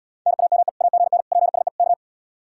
Morse code, heard as a keyed tone, about 76 wpm, 700 Hz. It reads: NAME AGN PSE K